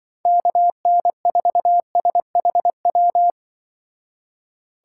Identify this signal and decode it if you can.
Morse code, heard as a keyed tone, about 24 wpm, 700 Hz. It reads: KN4SHW
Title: Speed Racing - USA 2x3 Call Signs 20wpm